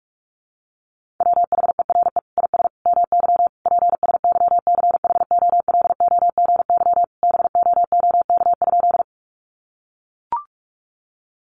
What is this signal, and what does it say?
Morse code, heard as a keyed tone, about 44 wpm, 700 Hz. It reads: WHERE IS MY PSYCHOLOGY BOOK?
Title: Sentences from Top 1400 Words 45wpm